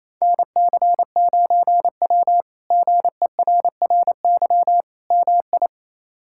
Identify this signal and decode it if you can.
Morse code, heard as a keyed tone, about 28 wpm, 700 Hz. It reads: NC9W GERRY MI
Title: K1USN SST QSO Practice - 28wpm